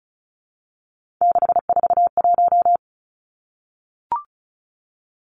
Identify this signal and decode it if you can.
Morse code, heard as a keyed tone, about 36 wpm, 700 Hz. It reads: 641
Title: Three Challenging Letter-Number Combinations 35wpm